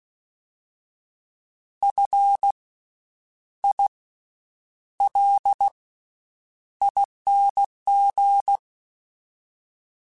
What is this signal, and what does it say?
Morse code, heard as a keyed tone, about 16 wpm, 800 Hz. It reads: F I L ING
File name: Special INGs-mp3s / Special INGs-mp3s / Slow ing @15